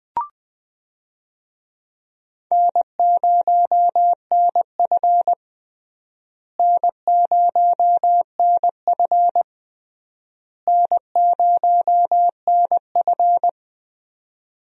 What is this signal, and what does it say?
Morse code, heard as a keyed tone, about 20 wpm, 700 Hz. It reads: N0NF N0NF N0NF
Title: USA Call Signs - 4 Characters - Repeated 3x - 20wpm